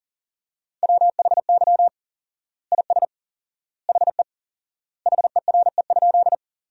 Morse code, heard as a keyed tone, about 40 wpm, 700 Hz. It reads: WHY IS HE HERE?
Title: 4x Word-Spacing - Sentences from Top 200 Words - 40wpm